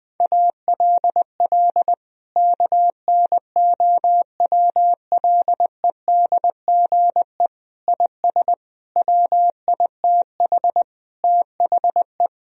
Morse code, heard as a keyed tone, about 20 wpm, 700 Hz. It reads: ALL KNOWLEDGE IS WITH THE
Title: Sentences from Top 1000 Words - Review Part 3 - 20wpm